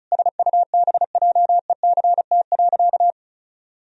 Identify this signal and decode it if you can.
Morse code, heard as a keyed tone, about 34 wpm, 700 Hz. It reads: SUBJECT.
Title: Sentences from Top 1000 Words - Review Part 1 - 35wpm